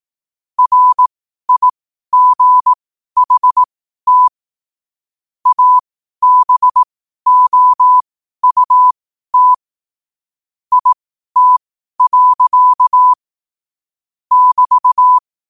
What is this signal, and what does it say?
Morse code, heard as a keyed tone, about 18 wpm, spaced as for 12 wpm, 1000 Hz. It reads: RIGHT ABOUT IT. <BT>
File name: qotdfiles / 12wpm